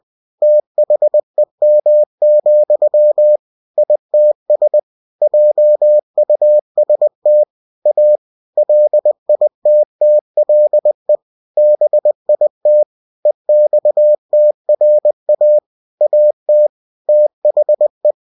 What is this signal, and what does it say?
Morse code, heard as a keyed tone, about 20 wpm, 600 Hz. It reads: THEM, ITS JUST A LITTLE BIT EXTRA AT THE
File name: cw-books / winnie-the-pooh / winnie-20wpm-page-0088